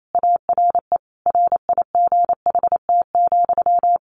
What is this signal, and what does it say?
Morse code, heard as a keyed tone, about 28 wpm, 700 Hz. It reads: ARE RIGHT,
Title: Sentences from Top 300 Words 28wpm